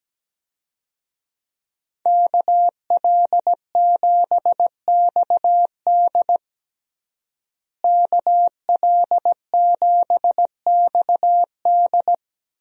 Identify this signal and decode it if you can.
Morse code, heard as a keyed tone, about 17 wpm, 700 Hz. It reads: KL7XD KL7XD
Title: DXCC-100 Call Signs - 5 Characters - Repeated 3x - 17wpm